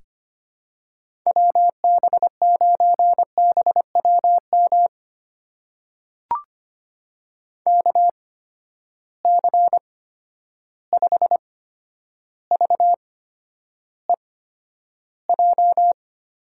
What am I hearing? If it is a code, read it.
Morse code, heard as a keyed tone, about 25 wpm, spaced as for 11 wpm, 700 Hz. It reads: WB9BWM KC5VEJ
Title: USA Call Signs - 6 Characters - 8x Character Spacing - 25wpm